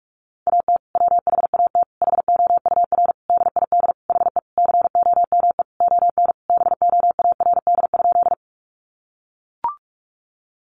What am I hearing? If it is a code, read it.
Morse code, heard as a keyed tone, about 44 wpm, 700 Hz. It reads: AT WHAT HOUR DID HE COME ON BOARD?
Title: Sentences from Top 700 Words 45wpm